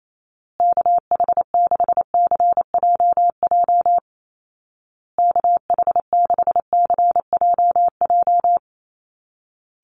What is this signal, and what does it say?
Morse code, heard as a keyed tone, about 28 wpm, 700 Hz. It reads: KH6CJJ KH6CJJ
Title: Call Signs Repeated 2x - 28wpm